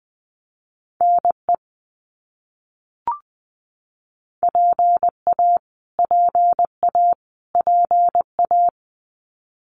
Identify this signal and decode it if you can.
Morse code, heard as a keyed tone, about 20 wpm, 700 Hz. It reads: NE PA PA PA
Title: US States Repeated 3x 20wpm